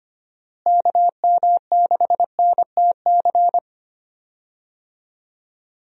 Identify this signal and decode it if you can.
Morse code, heard as a keyed tone, about 25 wpm, 700 Hz. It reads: KM6NTC
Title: USA Call Signs - 6 Characters - Repeated 3x - 25wpm